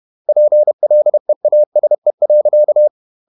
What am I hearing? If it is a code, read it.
Morse code, heard as a keyed tone, about 31 wpm, 600 Hz. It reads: PLEASE.
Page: Never 31wpm | CW med Gnister